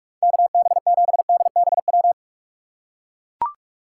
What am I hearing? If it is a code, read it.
Morse code, heard as a keyed tone, about 44 wpm, 700 Hz. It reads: K B 7 D B W
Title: USA Call Signs - 6 Characters - 6x Character Spacing - 45wpm